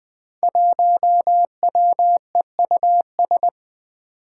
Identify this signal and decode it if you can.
Morse code, heard as a keyed tone, about 20 wpm, 700 Hz. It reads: 1WEUS
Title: Five Challenging Letter-Number Combinations 20wpm